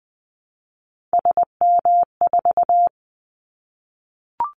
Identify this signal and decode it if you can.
Morse code, heard as a keyed tone, about 20 wpm, 700 Hz. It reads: SM4
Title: Callsign Prefixes for Top 25 DXCC - Repeated 3x - 20wpm